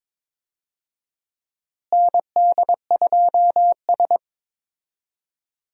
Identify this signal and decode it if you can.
Morse code, heard as a keyed tone, about 22 wpm, 700 Hz. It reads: ND2S
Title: USA Call Signs - 4 Characters - Repeated 3x - 22wpm